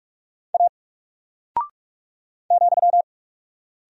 Morse code, heard as a keyed tone, about 44 wpm, 700 Hz. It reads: A ,
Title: Single Character (Letter:Number:Punctuation:Prosign) Alt 45wpm